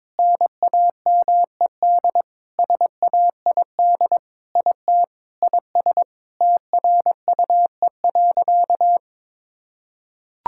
Morse code, heard as a keyed tone, about 22 wpm, 700 Hz. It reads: NAMED SAID IT IS TRUE.
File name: practice / 22wpm / Sentences with 7-Word Phrases - 22wpm